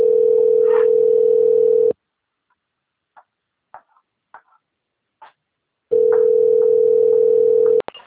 Ringback tone (North America), 440 + 480 Hz.